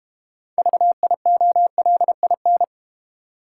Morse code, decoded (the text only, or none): VIOLIN